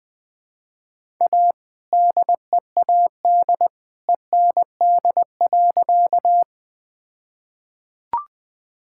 Morse code, heard as a keyed tone, about 21 wpm, 700 Hz. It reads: A DEAD END.
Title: All-Age Appropriate Jokes - 20wpm